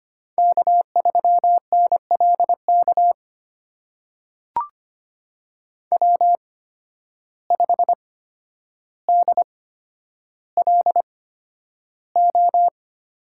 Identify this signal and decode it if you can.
Morse code, heard as a keyed tone, about 25 wpm, spaced as for 11 wpm, 700 Hz. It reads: K3NLK W5DLO
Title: USA Call Signs - 5 Characters - 8x Character Spacing - 25wpm